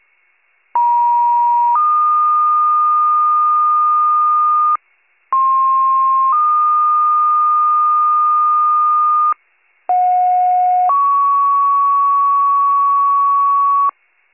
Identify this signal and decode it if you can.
Two-tone page, heard as a one-second tone, then a three-second tone, about 950 then 1220 Hz, about 1060 then 1220 Hz, about 730 then 1120 Hz. It is heard from 0.7 to 13.9 s.